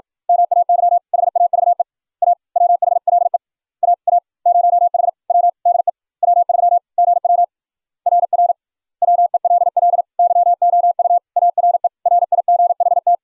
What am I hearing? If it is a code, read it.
Morse code, heard as a keyed tone, about 42 wpm, 700 Hz. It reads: MAY HAVE A WHLE AA0HW DE W3GW RR WELL YOU ARE RIGHT